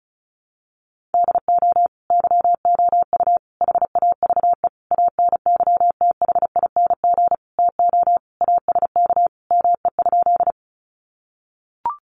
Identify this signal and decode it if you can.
Morse code, heard as a keyed tone, about 34 wpm, 700 Hz. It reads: DO YOU HAVE ANYTHING TO ASK ME?
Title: Sentences from Top 1000 Words 35wpm